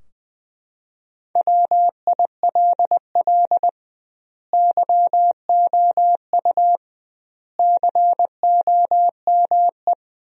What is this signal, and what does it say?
Morse code, heard as a keyed tone, about 20 wpm, 700 Hz. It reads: WILL YOU COME
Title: 2x Word-Spacing - Sentences from Top 200 Words - 20wpm